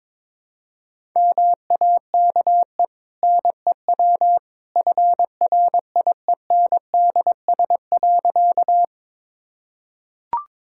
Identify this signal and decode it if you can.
Morse code, heard as a keyed tone, about 21 wpm, 690 Hz. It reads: MAKE NEW FRIENDS.
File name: practice / 22wpm / Easy Sentences with Hints - 22wpm